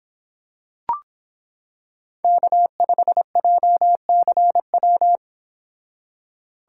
Morse code, heard as a keyed tone, about 26 wpm, 700 Hz. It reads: K5JCW